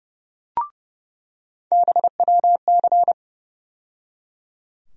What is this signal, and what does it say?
Morse code, heard as a keyed tone, about 32 wpm, 700 Hz. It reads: BWC
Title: USA Call Sign Postfixes - 3 Letters 30wpm